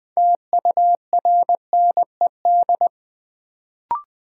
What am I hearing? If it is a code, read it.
Morse code, heard as a keyed tone, about 20 wpm, 700 Hz. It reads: TURNED
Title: Sets of 4 Words - New Word Drill - 20wpm